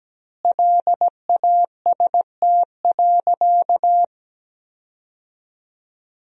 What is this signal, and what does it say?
Morse code, heard as a keyed tone, about 17 wpm, 700 Hz. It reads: LAST.
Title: 2x Word-Spacing - Sentences from Top 300 Words - 17wpm